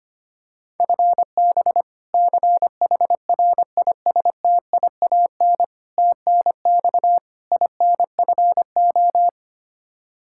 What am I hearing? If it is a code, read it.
Morse code, heard as a keyed tone, about 25 wpm, 700 Hz. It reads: FB CHRISTIAN TNX INFO
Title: K1USN SST QSO Practice 25wpm